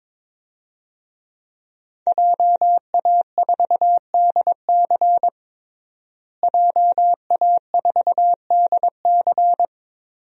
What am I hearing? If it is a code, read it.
Morse code, heard as a keyed tone, about 22 wpm, 700 Hz. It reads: JA4DC JA4DC
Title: DXCC-100 Call Signs - 5 Characters - Repeated 2x - 22wpm